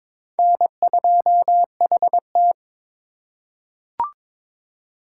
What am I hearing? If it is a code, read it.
Morse code, heard as a keyed tone, about 21 wpm, 700 Hz. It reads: N2HT
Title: USA Call Signs - 4 Characters - 4x Character Spacing - 22wpm